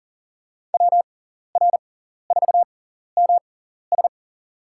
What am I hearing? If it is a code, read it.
Morse code, heard as a keyed tone, about 37 wpm, spaced as for 14 wpm, 700 Hz. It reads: WR4MS
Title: USA Call Signs - 6x Character Spacing - 40wpm